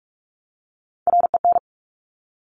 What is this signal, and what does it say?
Morse code, heard as a keyed tone, about 46 wpm, 700 Hz. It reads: REN
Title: Top Three Letter Combinations in Words 45wpm